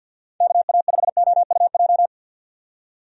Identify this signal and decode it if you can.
Morse code, heard as a keyed tone, about 46 wpm, 700 Hz. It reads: KA5OUJ
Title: USA Call Signs - 6 Characters - 50wpm